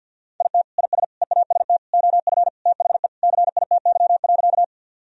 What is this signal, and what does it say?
Morse code, heard as a keyed tone, about 51 wpm, 700 Hz. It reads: IT IS EAST OF THE CITY.
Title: Sentences from Top 1000 Words - Review Part 3 - 50wpm